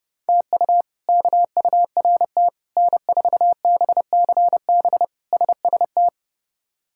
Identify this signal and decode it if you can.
Morse code, heard as a keyed tone, about 32 wpm, 700 Hz. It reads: TU KURT N4BCB SST
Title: K1USN SST QSO Practice - 30wpm